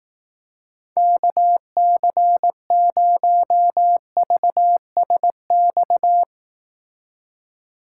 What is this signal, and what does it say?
Morse code, heard as a keyed tone, about 18 wpm, 700 Hz. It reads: KC0VSX
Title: Speed Racing - USA 2x3 Call Signs 15wpm